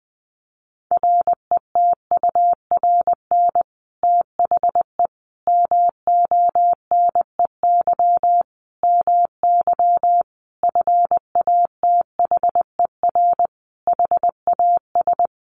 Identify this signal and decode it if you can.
Morse code, heard as a keyed tone, about 20 wpm, 700 Hz. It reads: RETURN THE MONEY MY FATHER HAS
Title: Sentences from Top 1200 Words 20wpm